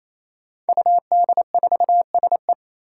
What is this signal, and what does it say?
Morse code, heard as a keyed tone, about 27 wpm, 700 Hz. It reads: UD4SE
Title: Five Challenging Letter-Number Combinations 28wpm